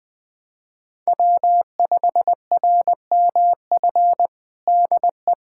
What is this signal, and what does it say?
Morse code, heard as a keyed tone, about 21 wpm, 700 Hz. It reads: W5RMF DE
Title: USA Callsign DE Callsign 20wpm